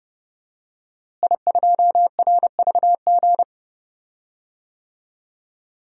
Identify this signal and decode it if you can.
Morse code, heard as a keyed tone, about 30 wpm, 700 Hz. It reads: I2RVG
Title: DXCC-100 Call Signs - 5 Characters - Repeated 3x - 30wpm